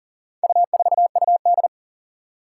Morse code, heard as a keyed tone, about 40 wpm, 700 Hz. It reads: U4UD
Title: Four Challenging Letter-Number Combinations 40wpm